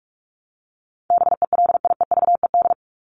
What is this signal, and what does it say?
Morse code, heard as a keyed tone, about 44 wpm, 700 Hz. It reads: BELIEVED
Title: ED Suffix - Top 100 Words with Base Words 45wpm